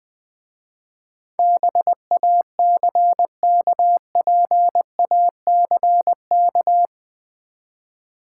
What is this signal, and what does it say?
Morse code, heard as a keyed tone, about 20 wpm, 700 Hz. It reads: BACKPACK
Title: Easy Words with Hints - 20wpm